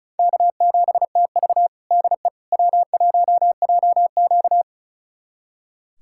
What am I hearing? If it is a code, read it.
Morse code, heard as a keyed tone, about 36 wpm, 700 Hz. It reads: K7TV DE W1JQ
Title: USA Callsign DE Callsign - 4 Characters - 35wpm